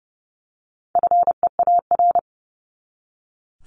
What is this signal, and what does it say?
Morse code, heard as a keyed tone, about 30 wpm, 700 Hz. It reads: FEAR
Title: Top 1751-2000 Words 30wpm